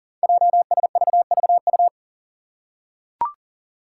Morse code, heard as a keyed tone, about 40 wpm, 700 Hz. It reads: JSVVU
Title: Five Challenging Letter-Number Combinations 40wpm